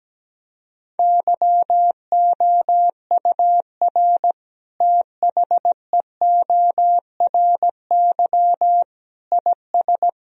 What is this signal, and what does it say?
Morse code, heard as a keyed tone, about 17 wpm, 700 Hz. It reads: YOUR THEORY IS
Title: Sentences from Top 1000 Words - Review Part 3 - 17wpm